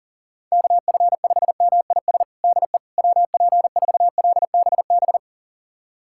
Morse code, heard as a keyed tone, about 40 wpm, 700 Hz. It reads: KF5MIS DE WP4LBB
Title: USA Callsign DE Callsign - 6 Characters - 40wpm